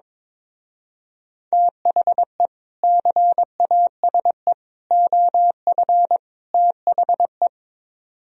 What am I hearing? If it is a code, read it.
Morse code, heard as a keyed tone, about 22 wpm, 700 Hz. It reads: THE CASE OF THE
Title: Speed-Racing - Sets of 4 Words 22wpm